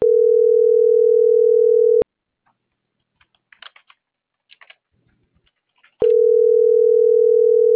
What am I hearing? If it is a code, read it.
Ringback tone (North America), 440 + 480 Hz.